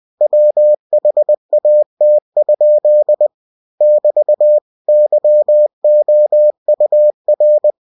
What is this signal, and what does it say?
Morse code, heard as a keyed tone, about 20 wpm, 600 Hz. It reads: WHAT? <BT> YOUR
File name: cw-books / winnie-the-pooh / winnie-20wpm-page-0061